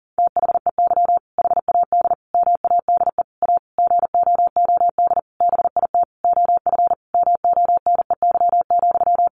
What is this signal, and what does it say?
Morse code, heard as a keyed tone, about 37 wpm, 700 Hz. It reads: THEY HAD MADE A GOOD BIT OF MONEY,
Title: Sentences from Top 900 Words 40wpm